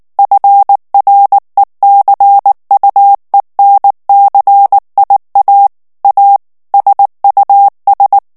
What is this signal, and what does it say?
Morse code, heard as a keyed tone, about 19 wpm, 800 Hz. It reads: FRECUENCIA A SUS
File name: morse mp3 / ESPAGNOL TEXTE EN CLAIR 2 VITESSE 18 MOTS MINUTES